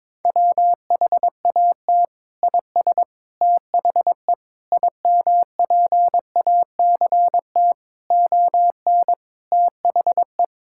Morse code, heard as a keyed tone, about 22 wpm, 700 Hz. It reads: WHAT IS THE IMPACT ON THE
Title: Sentences from Top 1000 Words - Review Part 6 - 22wpm